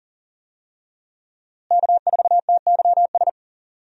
Morse code, heard as a keyed tone, about 39 wpm, 700 Hz. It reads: K4TYS
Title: USA Call Signs - 5 Characters - Repeated 2x - 40wpm